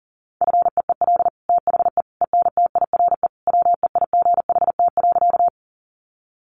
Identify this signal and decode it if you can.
Morse code, heard as a keyed tone, about 40 wpm, 690 Hz. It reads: FEEL THE ENTIRE WEIGHT.
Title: Sentences from Top 800 Words 40wpm